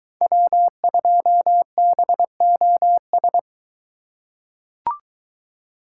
Morse code, heard as a keyed tone, about 22 wpm, 700 Hz. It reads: W2BOS